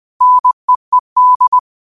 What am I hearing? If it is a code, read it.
Morse code, heard as a keyed tone, about 20 wpm, 1000 Hz. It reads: NEED
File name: qotdfiles / 20wpm